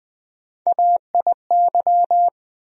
Morse code, heard as a keyed tone, about 21 wpm, 700 Hz. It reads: AIY
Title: USA Call Sign Postfixes - 3 Letters 20wpm